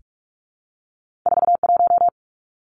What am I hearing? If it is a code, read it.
Morse code, heard as a keyed tone, about 46 wpm, 700 Hz. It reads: 41
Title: Numbers - 2-Digit 45wpm